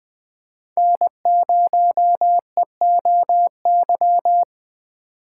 Morse code, heard as a keyed tone, about 20 wpm, 700 Hz. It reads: N0EOY
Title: USA Call Signs - 5 Characters - 6x Character Spacing - 20wpm